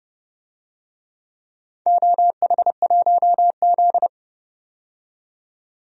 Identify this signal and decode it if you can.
Morse code, heard as a keyed tone, about 30 wpm, 700 Hz. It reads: OH1Z
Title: DXCC-100 Call Signs - 4 Characters - 30wpm